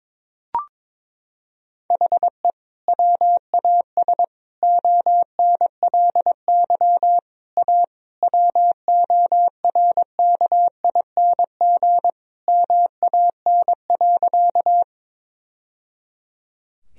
Morse code, heard as a keyed tone, about 22 wpm, 700 Hz. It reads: HE WAS ONLY A WORKING MAN.
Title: Sentences from Top 1000 Words - Review Part 4 - 22wpm